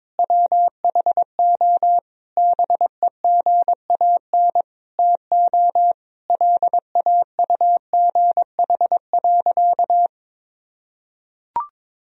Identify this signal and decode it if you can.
Morse code, heard as a keyed tone, about 22 wpm, 700 Hz. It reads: WHO BEGAN TO LAUGH.
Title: Sentences from Top 1000 Words - Review Part 1 - 22wpm